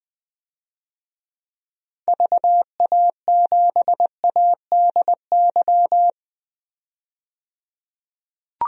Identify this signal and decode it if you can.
Morse code, heard as a keyed tone, about 20 wpm, 700 Hz. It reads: VA7ADY
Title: DXCC-100 Call Signs - 6 Characters - 20wpm